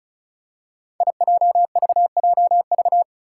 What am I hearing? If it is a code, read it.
Morse code, heard as a keyed tone, about 36 wpm, 700 Hz. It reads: IJVJV
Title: Five Challenging Letter-Number Combinations 35wpm